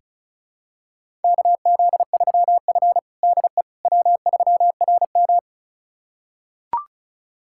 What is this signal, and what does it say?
Morse code, heard as a keyed tone, about 34 wpm, 700 Hz. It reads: KZ3F DE W3RM
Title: USA Callsign DE Callsign - 4 Characters - 35wpm